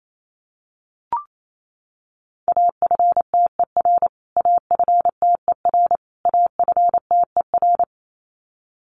Morse code, heard as a keyed tone, about 28 wpm, 700 Hz. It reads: AFTER AFTER AFTER
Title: Top 200 Words Repeated 28wpm